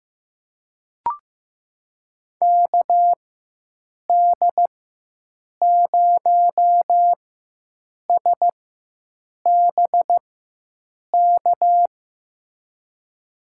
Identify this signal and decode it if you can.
Morse code, heard as a keyed tone, about 15 wpm, spaced as for 7 wpm, 700 Hz. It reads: KD0SBK
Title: USA Call Signs - 6 Characters - 4x Character Spacing - 15wpm